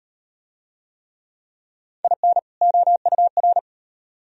Morse code, heard as a keyed tone, about 38 wpm, 700 Hz. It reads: IN OUR